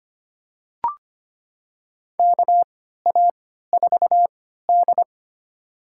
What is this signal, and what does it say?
Morse code, heard as a keyed tone, about 25 wpm, spaced as for 14 wpm, 700 Hz. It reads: KA4D
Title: USA Call Signs - 3x Character Spacing - 25wpm